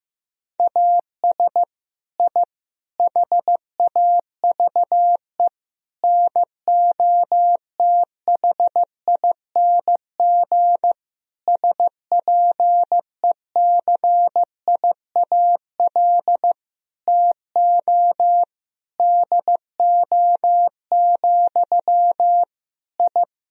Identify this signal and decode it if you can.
Morse code, heard as a keyed tone, about 15 wpm, 700 Hz. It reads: AS I HAVE NOTHING SPECIAL TO DO, I
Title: Sentences from Top 1000 Words - Review Part 3 - 15wpm